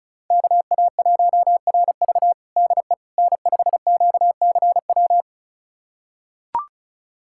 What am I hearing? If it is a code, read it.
Morse code, heard as a keyed tone, about 36 wpm, 700 Hz. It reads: KA1RV DE N5QCW
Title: USA Callsign DE Callsign - 5 Characters - 35wpm